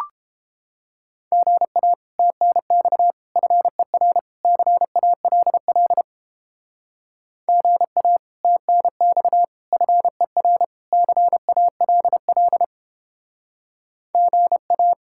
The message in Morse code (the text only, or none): GA TNX FER CALL GA TNX FER CALL GA